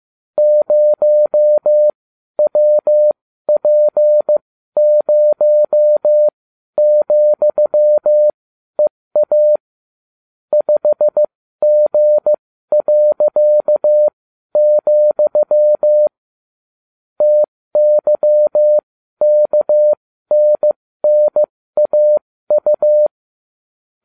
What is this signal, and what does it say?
Morse code, heard as a keyed tone, about 15 wpm, spaced as for 12 wpm, 600 Hz. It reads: 0WP0,EA 5G., TYKNNAU